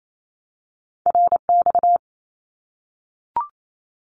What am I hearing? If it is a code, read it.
Morse code, heard as a keyed tone, about 28 wpm, 700 Hz. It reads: RX
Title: Abbreviations 28wpm